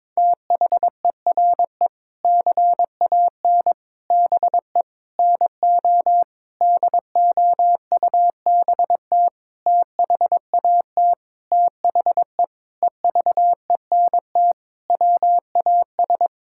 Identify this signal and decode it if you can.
Morse code, heard as a keyed tone, about 22 wpm, 700 Hz. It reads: THERE CAN BE NO DOUBT THAT THE EVENT WAS
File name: Sentences with 7-Word Phrases - 22wpm